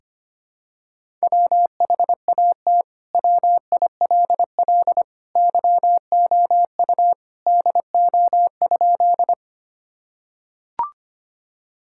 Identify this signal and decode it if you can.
Morse code, heard as a keyed tone, about 25 wpm, 700 Hz. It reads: WHAT WILL YOU DO?
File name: practice / 25wpm / Sentences with 2-Word Phrases - 25wpm